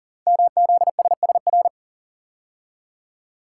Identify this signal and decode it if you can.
Morse code, heard as a keyed tone, about 39 wpm, 690 Hz. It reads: MZSSR
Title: Code Groups - 5 Letters - 40wpm